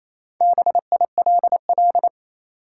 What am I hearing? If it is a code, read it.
Morse code, heard as a keyed tone, about 28 wpm, 700 Hz. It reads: BILL
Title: Four Letter Words - 28wpm